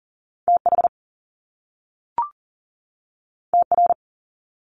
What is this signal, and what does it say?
Morse code, heard as a keyed tone, about 40 wpm, 700 Hz. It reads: TH TR
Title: Top Two Letter Combinations in Words 40wpm